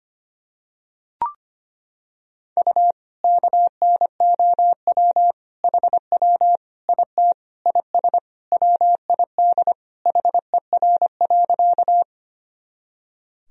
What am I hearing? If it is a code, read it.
Morse code, heard as a keyed tone, about 25 wpm, 700 Hz. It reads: U KNOW HW IT IS WID HER.
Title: Sentences from Top 100 Words and Abbreviations 25wpm